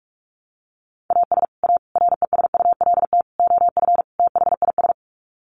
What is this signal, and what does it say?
Morse code, heard as a keyed tone, about 44 wpm, 700 Hz. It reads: AS A RESULT OF THIS
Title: Sets of 5 Words 45wpm